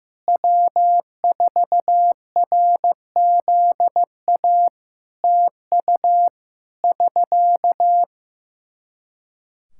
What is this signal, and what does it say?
Morse code, heard as a keyed tone, about 15 wpm, 700 Hz. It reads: W4RZA TU <SK>
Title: Practice QSO - Exchange 4 - Ending - 15wpm